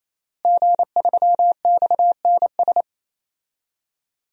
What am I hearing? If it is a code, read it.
Morse code, heard as a keyed tone, about 28 wpm, 700 Hz. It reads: G3XNS
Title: DXCC-100 Call Signs - 5 Characters - Repeated 3x - 28wpm